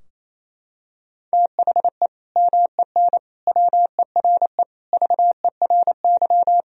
Morse code, heard as a keyed tone, about 28 wpm, 700 Hz. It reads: THE MEN WERE VERY
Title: Sentences from Top 1000 Words - Review Part 3 - 28wpm